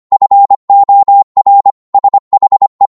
Morse code, heard as a keyed tone, about 25 wpm, 800 Hz. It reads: FOR SHE